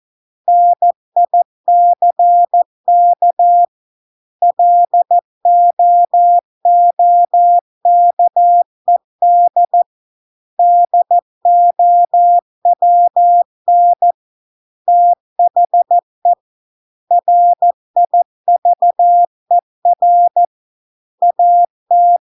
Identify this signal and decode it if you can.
Morse code, heard as a keyed tone, about 14 wpm, 700 Hz. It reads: NICK LOOKED DOWN THE RIVER AT